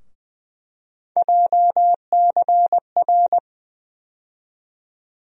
Morse code, heard as a keyed tone, about 20 wpm, 700 Hz. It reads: JCR
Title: USA Call Sign Postfixes - 3 Letters 20wpm